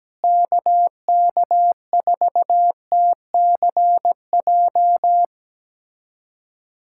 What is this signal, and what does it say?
Morse code, heard as a keyed tone, about 17 wpm, 700 Hz. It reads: KK4TCJ
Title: USA Callsign DE Callsign - 6 Characters - 17wpm